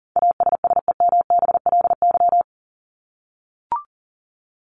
Morse code, heard as a keyed tone, about 38 wpm, 700 Hz. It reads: ASSEMBLY
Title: LY Suffix - Top 100 Words with Base Words 40wpm